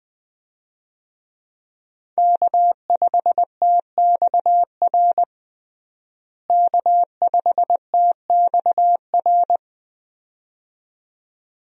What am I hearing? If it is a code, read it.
Morse code, heard as a keyed tone, about 20 wpm, 700 Hz. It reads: K5TXR K5TXR
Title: USA Call Signs - 5 Characters - Repeated 2x - 20wpm